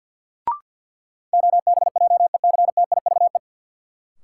Morse code, heard as a keyed tone, about 52 wpm, 700 Hz. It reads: OBJECTIVE
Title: Sentences from Top 1200 Words - New Word Drill Alt - 50wpm